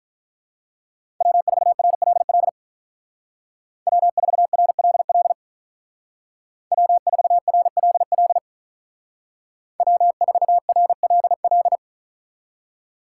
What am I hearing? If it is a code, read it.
Morse code, heard as a keyed tone, about 43 wpm, 700 Hz. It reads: W4RLL W4RLL W4RLL W4RLL
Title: Speed Racing - USA 1x3 Call Signs 35wpm